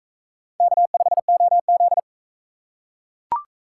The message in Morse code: K5OZ

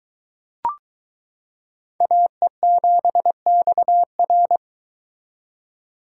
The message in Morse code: AE7XR

